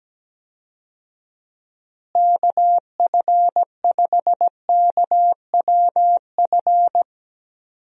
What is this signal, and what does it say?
Morse code, heard as a keyed tone, about 17 wpm, 700 Hz. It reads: KF5KWF
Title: USA Call Signs - 6 Characters - Repeated 2x - 17wpm